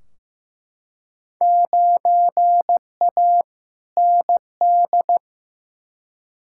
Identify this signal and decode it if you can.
Morse code, heard as a keyed tone, about 15 wpm, 700 Hz. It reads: 9A ND
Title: Field Day Exchanges Encore - 15wpm